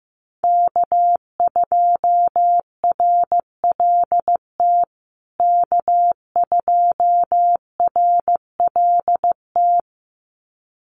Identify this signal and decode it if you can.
Morse code, heard as a keyed tone, about 15 wpm, 700 Hz. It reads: K2RLT K2RLT
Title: USA Call Signs Repeated 2x 15wpm